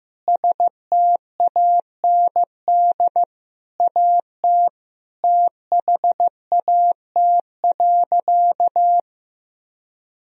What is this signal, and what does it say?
Morse code, heard as a keyed tone, about 15 wpm, 700 Hz. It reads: STAND AT THAT.
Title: Sentences from Top 1000 Words - Review Part 1 - 15wpm